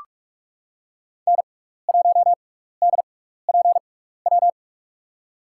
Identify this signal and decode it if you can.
Morse code, heard as a keyed tone, about 44 wpm, spaced as for 15 wpm, 700 Hz. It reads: N1DPW
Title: USA Call Signs - 5 Characters - 6x Character Spacing - 45wpm